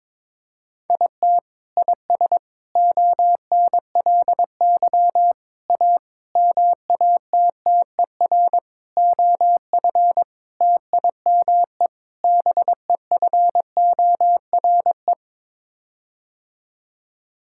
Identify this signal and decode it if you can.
Morse code, heard as a keyed tone, about 22 wpm, 700 Hz. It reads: IT IS ONLY A MATTER OF TIME BEFORE
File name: practice / 22wpm / Sets of 8 Words - 22wpm